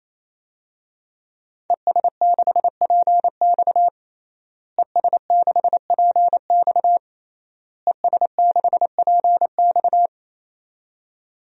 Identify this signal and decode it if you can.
Morse code, heard as a keyed tone, about 28 wpm, 700 Hz. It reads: ES6PX ES6PX ES6PX
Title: DXCC-100 Call Signs - 5 Characters - Repeated 3x - 28wpm